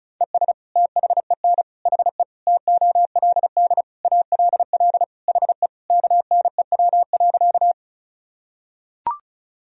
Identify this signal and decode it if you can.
Morse code, heard as a keyed tone, about 36 wpm, 700 Hz. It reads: ES THEN HE TOLD ALL HE KNEW.
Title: Sentences from Top 400 Words and Abbreviations 35wpm